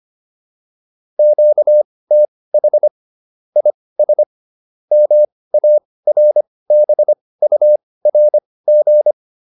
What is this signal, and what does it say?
Morse code, heard as a keyed tone, about 25 wpm, spaced as for 18 wpm, 600 Hz. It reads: QTH IS MARBURG
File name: QSO405 18